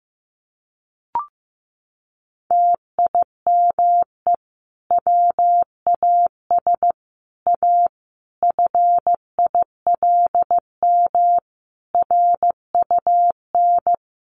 Morse code, heard as a keyed tone, about 15 wpm, 700 Hz. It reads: TIME WAS A FILM RUN
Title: A Sound Of Thunder 15wpm